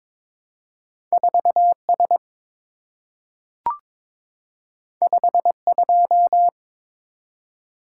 Morse code, heard as a keyed tone, about 22 wpm, 700 Hz. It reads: 4S 52
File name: practice / 22wpm / Two Letter-Number Combinations - 22wpm